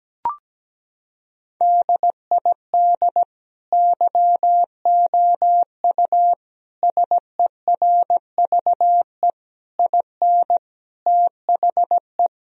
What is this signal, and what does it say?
Morse code, heard as a keyed tone, about 17 wpm, 700 Hz. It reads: DID YOU SERVE IN THE
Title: Sentences from Top 1000 Words - Review Part 4 - 17wpm